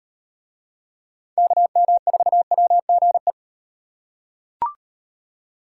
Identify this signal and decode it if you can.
Morse code, heard as a keyed tone, about 38 wpm, 700 Hz. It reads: KM4WGE